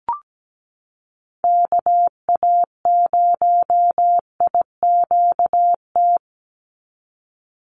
Morse code, heard as a keyed tone, about 17 wpm, 700 Hz. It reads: KA0IQT
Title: Call Signs - 17wpm